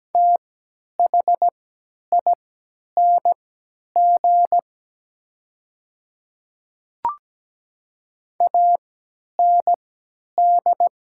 Morse code, heard as a keyed tone, about 17 wpm, spaced as for 10 wpm, 700 Hz. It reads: THING AND
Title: 3x Character Spacing - Top 300 Words - 17wpm